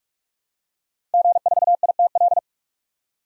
Morse code, heard as a keyed tone, about 46 wpm, 700 Hz. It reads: G4ITL
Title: DXCC-100 Call Signs - 5 Characters - 45wpm